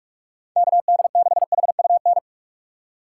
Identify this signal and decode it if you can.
Morse code, heard as a keyed tone, about 43 wpm, 700 Hz. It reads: KD6HUN
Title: USA Call Signs - 6x Character Spacing - 45wpm